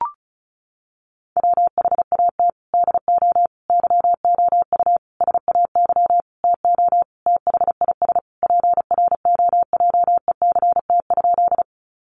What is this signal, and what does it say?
Morse code, heard as a keyed tone, about 34 wpm, 700 Hz. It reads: WHAT DO YOU SAY TO THIS PROJECT?
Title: Sentences from Top 1400 Words 35wpm